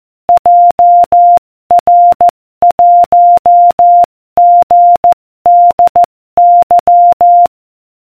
Morse code, heard as a keyed tone, about 14 wpm, 700 Hz. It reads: JR1GDY